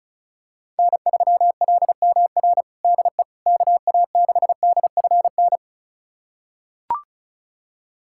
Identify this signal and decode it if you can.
Morse code, heard as a keyed tone, about 34 wpm, 700 Hz. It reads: N3LMR DE KA6DFN
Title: USA Callsign DE Callsign 35wpm